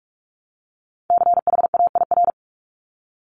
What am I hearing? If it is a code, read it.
Morse code, heard as a keyed tone, about 46 wpm, 700 Hz. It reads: CHAIR
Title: Multi-letter Phonemes Repeated 3x 45wpm